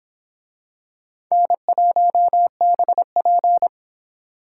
Morse code, heard as a keyed tone, about 22 wpm, 700 Hz. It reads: N1BP